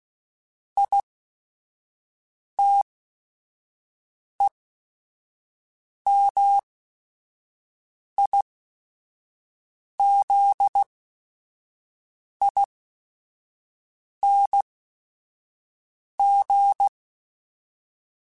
Morse code, heard as a keyed tone, about 16 wpm, spaced as for 5 wpm, 800 Hz. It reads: ITEMIZING
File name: Special INGs / Special INGs-mp3s / Sngl Ing @5